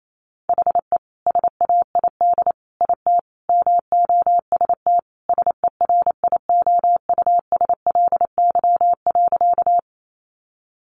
Morse code, heard as a keyed tone, about 28 wpm, 700 Hz. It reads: HE SAID IT MOST SERIOUSLY.
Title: Sentences from Top 1200 Words 28wpm